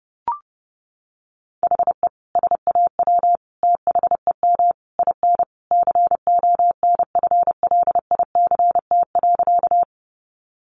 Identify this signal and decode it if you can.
Morse code, heard as a keyed tone, about 30 wpm, 700 Hz. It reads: HE SAW THEM IN CONFLICT.
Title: Sentences from Top 1300 Words 30wpm